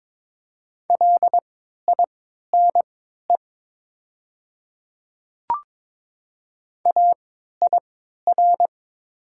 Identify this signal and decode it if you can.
Morse code, heard as a keyed tone, about 22 wpm, spaced as for 13 wpm, 700 Hz. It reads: LINE AIR
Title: 3x Character Spacing - Top 300 Words - 22wpm